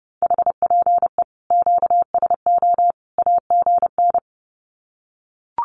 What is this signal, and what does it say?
Morse code, heard as a keyed tone, about 30 wpm, 700 Hz. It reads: HPE QSO AGN
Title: Common QSO Phrases 30wpm